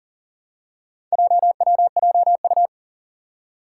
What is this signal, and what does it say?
Morse code, heard as a keyed tone, about 40 wpm, 700 Hz. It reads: JWJU
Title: Four Challenging Letter-Number Combinations 40wpm